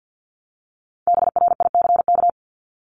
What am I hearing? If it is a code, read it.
Morse code, heard as a keyed tone, about 48 wpm, 700 Hz. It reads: BRICK
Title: Top 1751-2000 Words 50wpm